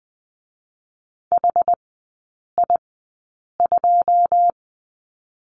Morse code, heard as a keyed tone, about 20 wpm, spaced as for 8 wpm, 690 Hz. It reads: HI2
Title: 081 Lesson 14e - Number 2 - Three Characters 20wpm